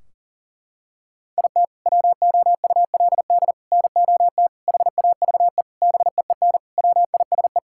Morse code, heard as a keyed tone, about 40 wpm, 700 Hz. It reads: IT WOULD NOT HAVE BEEN WISE
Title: Sentences from Top 1600 Words 40wpm